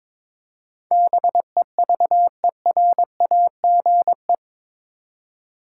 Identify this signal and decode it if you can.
Morse code, heard as a keyed tone, about 22 wpm, 700 Hz. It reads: BEVERAGE
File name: practice / 22wpm / Eight Letter Words - 22wpm